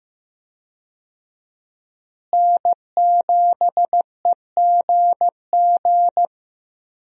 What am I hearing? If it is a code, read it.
Morse code, heard as a keyed tone, about 15 wpm, 700 Hz. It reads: N7EGG
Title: USA Call Signs - 5 Characters - Repeated 3x - 15wpm